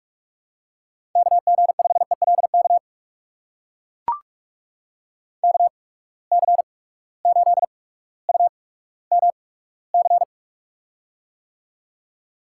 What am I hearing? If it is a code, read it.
Morse code, heard as a keyed tone, about 44 wpm, spaced as for 19 wpm, 700 Hz. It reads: KG5ELK KC8UMC
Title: USA Call Signs - 6 Characters - 8x Character Spacing - 45wpm